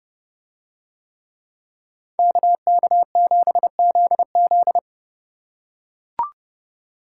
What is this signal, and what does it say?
Morse code, heard as a keyed tone, about 31 wpm, 700 Hz. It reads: KK7ZZ